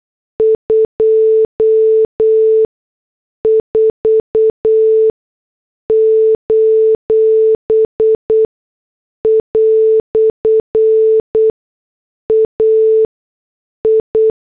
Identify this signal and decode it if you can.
Morse code, heard as a keyed tone, about 8 wpm, spaced as for 6 wpm, 440 Hz. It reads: 24: "AI
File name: morse / headlines